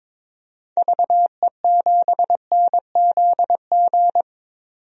Morse code, heard as a keyed tone, about 22 wpm, 700 Hz. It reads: VE7NZG
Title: DXCC-100 Call Signs - 6 Characters - 22wpm